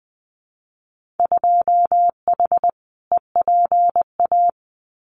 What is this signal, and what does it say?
Morse code, heard as a keyed tone, about 20 wpm, 700 Hz. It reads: 2H EPA
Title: Winter Field Day Exchanges Encore 20wpm